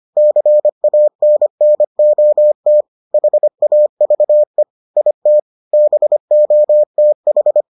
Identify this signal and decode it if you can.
Morse code, heard as a keyed tone, about 25 wpm, 600 Hz. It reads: CANNOT HAVE IT BOTH